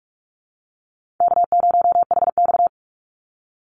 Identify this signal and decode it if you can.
Morse code, heard as a keyed tone, about 44 wpm, 700 Hz. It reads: K0HX
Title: Call Signs Repeated 2x - 45wpm